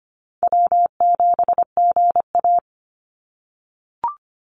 Morse code, heard as a keyed tone, about 25 wpm, 700 Hz. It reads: W7GA
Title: NA SOTA Chaser Callsigns 25wpm